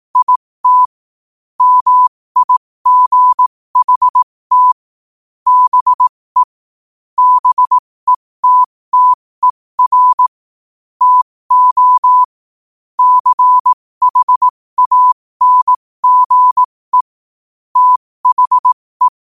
Morse code, heard as a keyed tone, about 18 wpm, spaced as for 15 wpm, 1000 Hz. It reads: IT MIGHT BE BETTER TO CHANGE THE
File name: qotdfiles / 15wpm